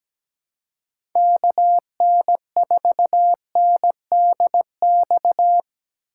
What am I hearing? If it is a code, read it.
Morse code, heard as a keyed tone, about 17 wpm, 700 Hz. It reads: KN4NDX
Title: USA Call Signs - 6 Characters - 2x Character Spacing - 17wpm